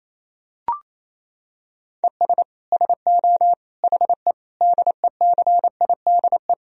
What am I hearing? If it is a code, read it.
Morse code, heard as a keyed tone, about 28 wpm, 700 Hz. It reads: ES SO HE DECIDE
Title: Sentences from Top 500 Words and Abbreviations 28wpm